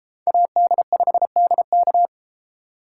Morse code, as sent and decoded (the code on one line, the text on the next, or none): .- -.. ..... -.. -.-
AD5DK